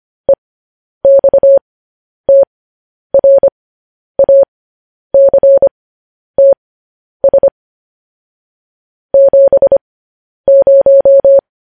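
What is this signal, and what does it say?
Morse code, heard as a keyed tone, about 25 wpm, spaced as for 10 wpm, 550 Hz. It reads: EXTRACTS 70
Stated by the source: Morse code news headlines sent at 10 words per minute.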